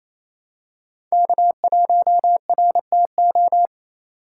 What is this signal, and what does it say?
Morse code, heard as a keyed tone, about 28 wpm, 700 Hz. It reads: K1RTO
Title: USA Call Signs - 2x Character Spacing - 28wpm